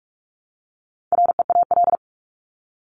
Morse code, heard as a keyed tone, about 44 wpm, 700 Hz. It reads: REAL
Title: Top 500 Words Repeated 45wpm